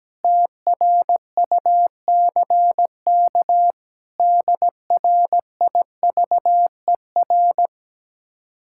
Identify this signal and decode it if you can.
Morse code, heard as a keyed tone, about 17 wpm, 700 Hz. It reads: TRUCK DRIVER
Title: Speed-Racing Common QSO Phrases - 17wpm